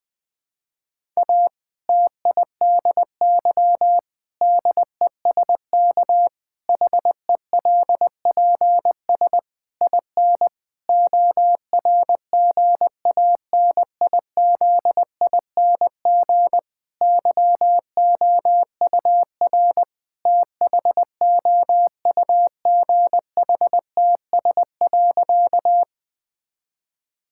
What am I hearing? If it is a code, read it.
Morse code, heard as a keyed tone, about 20 wpm, 700 Hz. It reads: A TIDY DESK HELPS IN ORGANIZING YOUR THOUGHTS.
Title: Difficult Sentences with Hints - 20wpm